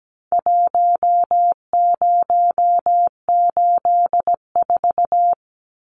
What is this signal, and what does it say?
Morse code, heard as a keyed tone, about 17 wpm, 700 Hz. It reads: 1084
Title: Numbers - Contest Style - 4-Digit 17wpm